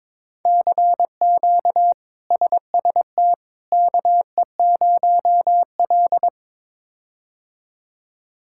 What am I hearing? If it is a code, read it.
Morse code, heard as a keyed tone, about 22 wpm, 700 Hz. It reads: CQ SST KE0L
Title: K1USN SST QSO Practice - 22wpm